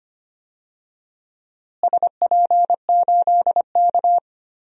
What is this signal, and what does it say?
Morse code, heard as a keyed tone, about 25 wpm, 700 Hz. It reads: SP8K
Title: DXCC-100 Call Signs - 4 Characters - Repeated 3x - 25wpm